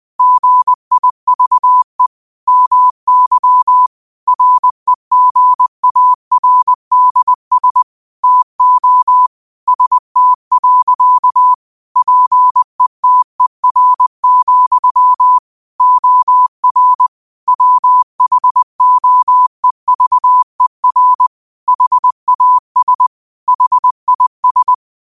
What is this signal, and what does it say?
Morse code, heard as a keyed tone, about 20 wpm, 1000 Hz. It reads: GIVE MY REGARDS TO ST. PETER, OR WHOEVER HAS HIS